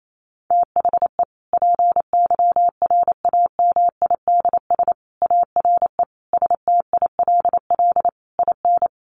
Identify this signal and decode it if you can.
Morse code, heard as a keyed tone, about 28 wpm, 700 Hz. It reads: THE PYRAMIDS ARE STILL IN
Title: A Sound Of Thunder 28wpm